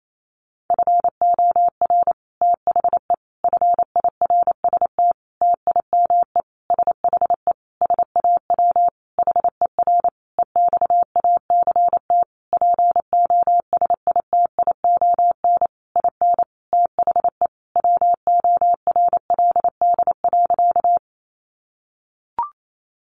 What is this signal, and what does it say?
Morse code, heard as a keyed tone, about 28 wpm, 700 Hz. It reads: FOR THE FIRST TIME SHE SAW HER EXACT POSITION IN THE WORLD.
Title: Sentences from Top 1300 Words 28wpm